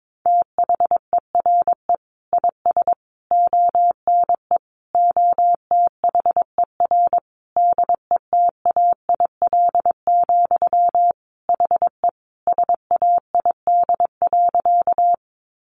Morse code, heard as a keyed tone, about 22 wpm, 700 Hz. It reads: THERE IS ONE OTHER DETAIL, HE SAID.
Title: Sentences from Top 1500 Words 22wpm